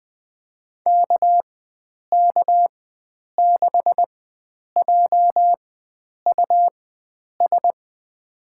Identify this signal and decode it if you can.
Morse code, heard as a keyed tone, about 20 wpm, spaced as for 9 wpm, 700 Hz. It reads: KK6JUS
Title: USA Call Signs - 4x Character Spacing - 20wpm